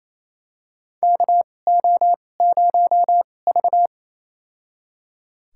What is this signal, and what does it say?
Morse code, heard as a keyed tone, about 27 wpm, spaced as for 20 wpm, 700 Hz. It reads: KO0V
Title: USA Call Signs - 4 Characters - 2x Character Spacing - 28wpm